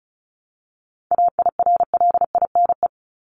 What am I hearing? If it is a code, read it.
Morse code, heard as a keyed tone, about 34 wpm, 700 Hz. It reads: AIRLINE
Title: Top 1501-1750 Words 35wpm